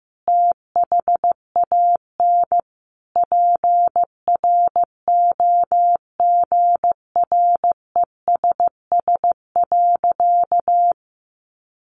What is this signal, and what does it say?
Morse code, heard as a keyed tone, about 15 wpm, 700 Hz. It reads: THAN PROGRESS.
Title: Sentences from Top 1400 Words 15wpm